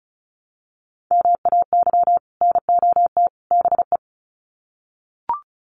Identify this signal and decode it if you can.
Morse code, heard as a keyed tone, about 34 wpm, 700 Hz. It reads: MAY NOT BE
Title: Sets of 3 Words 35wpm